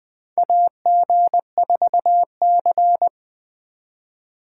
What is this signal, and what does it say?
Morse code, heard as a keyed tone, about 20 wpm, 700 Hz. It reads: AG4C
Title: USA Call Signs - 4 Characters - 4x Character Spacing - 20wpm